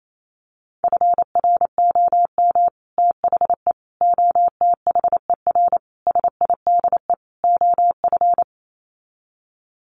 Morse code, heard as a keyed tone, about 28 wpm, 700 Hz. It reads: FROM THE OTHER SIDE OF
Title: Sets of 5 Words Encore 28wpm